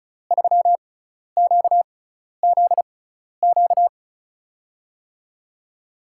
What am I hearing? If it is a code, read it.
Morse code, heard as a keyed tone, about 34 wpm, spaced as for 12 wpm, 700 Hz. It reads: 3QZQ
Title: USA Call Signs - 5 Characters - 6x Character Spacing - 35wpm